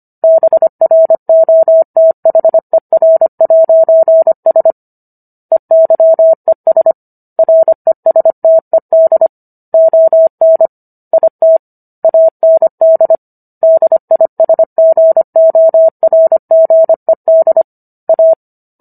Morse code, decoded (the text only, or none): BROTHER'S EYES RESTED ON IT AND DISGORGED A